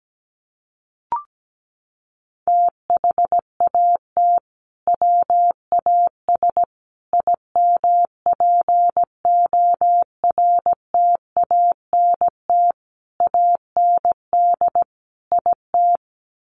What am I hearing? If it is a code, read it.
Morse code, heard as a keyed tone, about 17 wpm, 700 Hz. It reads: THAT WAS IMPORTANT AND IT
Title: Sentences from Top 600 Words 17wpm